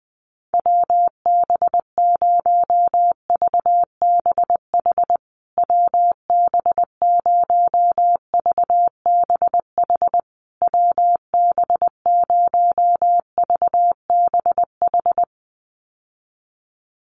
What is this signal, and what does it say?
Morse code, heard as a keyed tone, about 20 wpm, 700 Hz. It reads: WB0VBH WB0VBH WB0VBH
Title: USA Call Signs Repeated 3x 20wpm